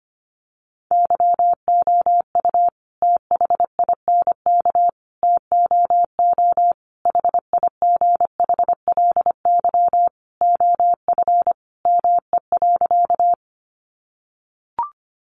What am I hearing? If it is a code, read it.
Morse code, heard as a keyed tone, about 25 wpm, 700 Hz. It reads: YOU THINK TOO HIGHLY OF ME.
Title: Sentences from Top 1100 Words 25wpm